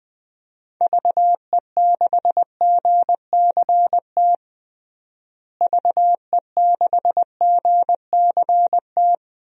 Morse code, heard as a keyed tone, about 20 wpm, 700 Hz. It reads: VE6GCT VE6GCT
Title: DXCC-100 Call Signs - 6 Characters - Repeated 2x - 20wpm